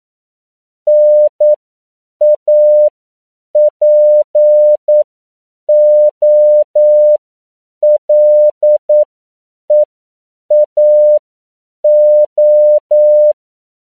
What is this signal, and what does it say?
Morse code, heard as a keyed tone, about 9 wpm, spaced as for 7 wpm, 600 Hz. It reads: NAPOLEAO